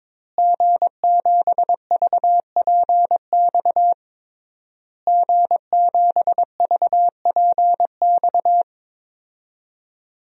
Morse code, heard as a keyed tone, about 22 wpm, 700 Hz. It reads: G7VPX G7VPX
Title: DXCC-100 Call Signs - 5 Characters - Repeated 3x - 22wpm